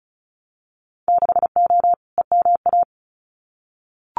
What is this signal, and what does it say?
Morse code, heard as a keyed tone, about 34 wpm, 700 Hz. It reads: 6O EMA
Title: Winter Field Day Exchanges Encore 35wpm